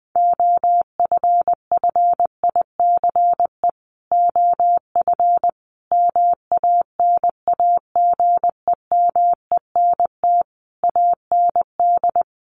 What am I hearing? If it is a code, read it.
Morse code, heard as a keyed tone, about 20 wpm, 700 Hz. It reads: OFFICE OF MANAGEMENT AND
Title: Sets of 5 Words Encore2 20wpm